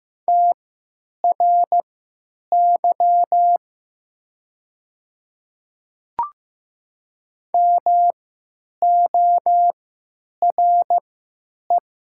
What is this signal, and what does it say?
Morse code, heard as a keyed tone, about 15 wpm, spaced as for 9 wpm, 700 Hz. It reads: TRY MORE
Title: 3x Character Spacing - Top 200 Words - 15wpm